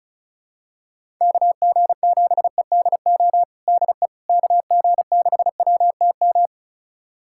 Morse code, heard as a keyed tone, about 36 wpm, 700 Hz. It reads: KG7EDO DE KG6WTM
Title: USA Callsign DE Callsign 35wpm